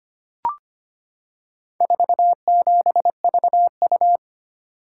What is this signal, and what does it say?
Morse code, heard as a keyed tone, about 25 wpm, 700 Hz. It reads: 47VU
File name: practice / 25wpm / Four Letter-Number Combinations - 25wpm